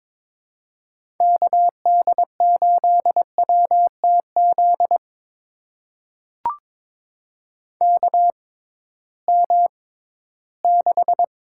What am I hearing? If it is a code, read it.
Morse code, heard as a keyed tone, about 22 wpm, spaced as for 14 wpm, 700 Hz. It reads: KD8WTZ KM6
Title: USA Call Signs - 6 Characters - 6x Character Spacing - 22wpm